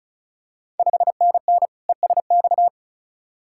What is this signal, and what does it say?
Morse code, heard as a keyed tone, about 36 wpm, 690 Hz. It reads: 5NN ESX
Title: Ontario QSO Party Exchange 35wpm